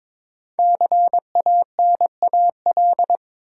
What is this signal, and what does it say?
Morse code, heard as a keyed tone, about 23 wpm, 700 Hz. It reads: CANAL
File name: 5 Letter Words - First 3 Letters Given - 22wpm